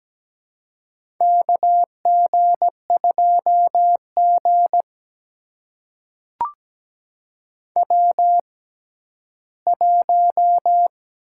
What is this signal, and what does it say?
Morse code, heard as a keyed tone, about 17 wpm, spaced as for 12 wpm, 700 Hz. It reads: KG2G W1
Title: USA Call Signs - 4 Characters - 6x Character Spacing - 17wpm